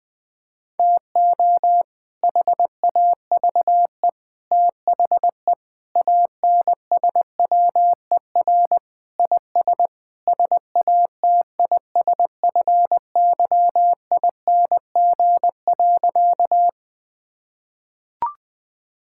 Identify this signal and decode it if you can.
Morse code, heard as a keyed tone, about 20 wpm, 700 Hz. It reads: TO HAVE THE ANSWER IS SATISFYING.
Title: Sentences with 3-Word Phrases - 20wpm